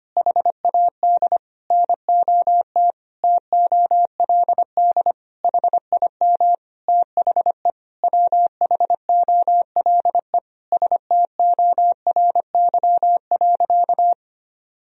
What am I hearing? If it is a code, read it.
Morse code, heard as a keyed tone, about 25 wpm, 700 Hz. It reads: HAD NOT TOLD HIM THE WHOLE STORY.
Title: Sentences from Top 1000 Words - Review Part 1 - 25wpm